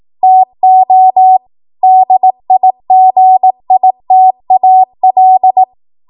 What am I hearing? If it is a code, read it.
Morse code, heard as a keyed tone, about 18 wpm, 750 Hz. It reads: TO DIGITAL
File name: Morse / 18WPM